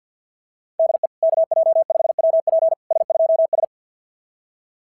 Morse code, heard as a keyed tone, about 46 wpm, 660 Hz. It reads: DE KJ5WP S2S
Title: SOTA QSO Practice 50wpm